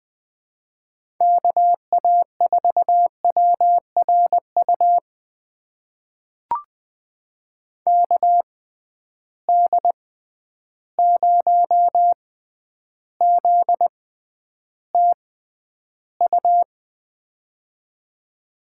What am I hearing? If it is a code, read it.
Morse code, heard as a keyed tone, about 20 wpm, spaced as for 10 wpm, 700 Hz. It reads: KA4WRU KD0ZTU
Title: USA Call Signs - 6 Characters - 6x Character Spacing - 20wpm